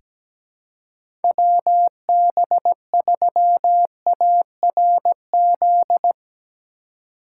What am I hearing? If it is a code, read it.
Morse code, heard as a keyed tone, about 17 wpm, 700 Hz. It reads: WB3ARZ
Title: USA Call Signs - 6 Characters - 3x Character Spacing - 17wpm